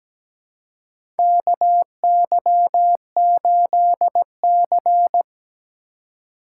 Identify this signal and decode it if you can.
Morse code, heard as a keyed tone, about 17 wpm, 700 Hz. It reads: KY8C
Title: USA Call Signs - 4 Characters - 4x Character Spacing - 17wpm